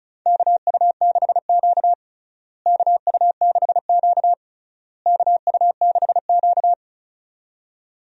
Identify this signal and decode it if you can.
Morse code, heard as a keyed tone, about 36 wpm, 700 Hz. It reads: KU6Q KU6Q KU6Q
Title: USA Call Signs - 4 Characters - Repeated 3x - 35wpm